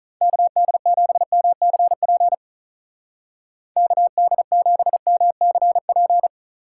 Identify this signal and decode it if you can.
Morse code, heard as a keyed tone, about 42 wpm, 700 Hz. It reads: KD7MCP KD7MCP
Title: Speed Racing - USA 2x3 Call Signs 35wpm